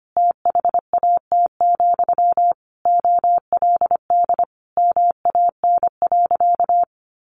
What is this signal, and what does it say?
Morse code, heard as a keyed tone, about 25 wpm, 700 Hz. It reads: THAT, OLD MAN.
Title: Sentences from Top 800 Words 25wpm